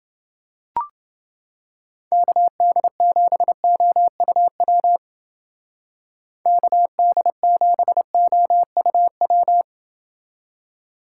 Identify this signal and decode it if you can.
Morse code, heard as a keyed tone, about 29 wpm, 700 Hz. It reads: KD7OUW KD7OUW